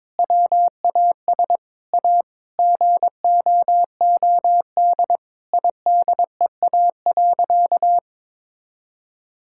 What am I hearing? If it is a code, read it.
Morse code, heard as a keyed tone, about 22 wpm, 700 Hz. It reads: WAS A GOOD IDEA.
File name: practice / 22wpm / Sentences with 4-Word Phrases - 22wpm